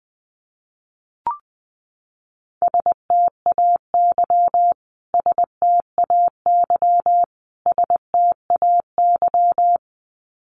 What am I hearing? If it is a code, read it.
Morse code, heard as a keyed tone, about 20 wpm, 700 Hz. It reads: STAY STAY STAY
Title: Top 500 Words Repeated 20wpm